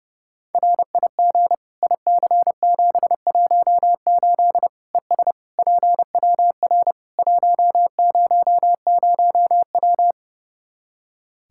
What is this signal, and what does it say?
Morse code, heard as a keyed tone, about 29 wpm, 700 Hz. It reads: RIG IC718 ES PWR 100W
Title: Practice QSO - Exchange 2 - Info - 30wpm